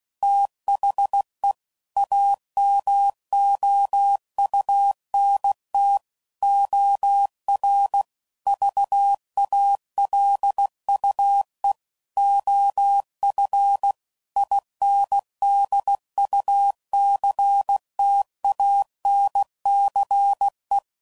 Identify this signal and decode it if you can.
Morse code, heard as a keyed tone, about 16 wpm, 800 Hz. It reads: THE AMOUNT OR VALUE OF INDUCTANCE
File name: CW Code Groups @15WPM/Technical Subjects